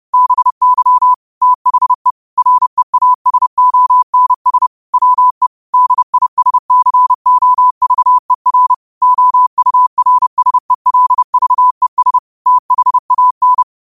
Morse code, heard as a keyed tone, about 30 wpm, 1000 Hz. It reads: BY THE REASONS WE DISCOVER OURSELVES THAN